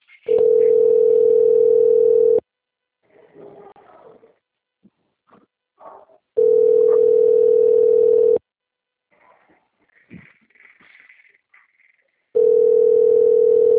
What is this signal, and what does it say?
Ringback tone (North America), 440 + 480 Hz.